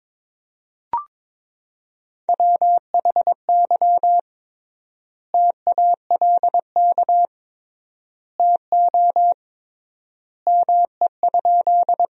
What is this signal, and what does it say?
Morse code, heard as a keyed tone, about 22 wpm, 700 Hz. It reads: WHY TALK TO ME?
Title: 3x Word-Spacing - Sentences from Top 400 Words - 22wpm